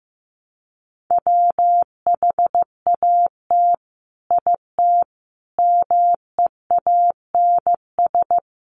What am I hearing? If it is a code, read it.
Morse code, heard as a keyed tone, about 15 wpm, 700 Hz. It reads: WHAT IT MEANS
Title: Sets of 5 Words 15wpm